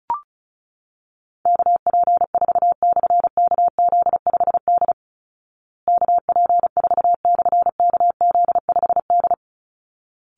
Morse code, heard as a keyed tone, about 34 wpm, 700 Hz. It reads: KP4/KZ5D KP4/KZ5D
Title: Call Signs Repeated 3x - 35wpm